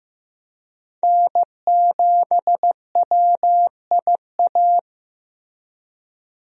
Morse code, heard as a keyed tone, about 15 wpm, 700 Hz. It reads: N7WIA
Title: USA Call Signs - 5 Characters - 15wpm